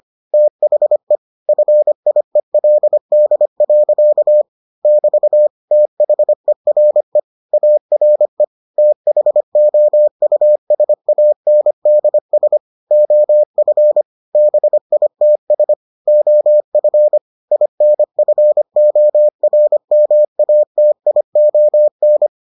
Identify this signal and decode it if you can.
Morse code, heard as a keyed tone, about 25 wpm, 600 Hz. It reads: THE FIELD. <BT> THERE ARE THOUSANDS OF BITS OF INFORMATION